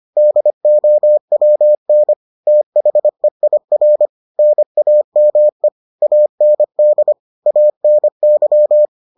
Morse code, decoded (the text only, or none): DOWN THEIR NAME AND ANY